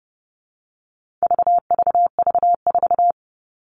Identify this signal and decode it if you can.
Morse code, heard as a keyed tone, about 30 wpm, 700 Hz. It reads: VVV4
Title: All Cut Numbers - 4-Digit 30wpm